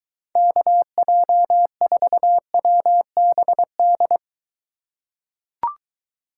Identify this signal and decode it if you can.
Morse code, heard as a keyed tone, about 23 wpm, 700 Hz. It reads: KJ4WBD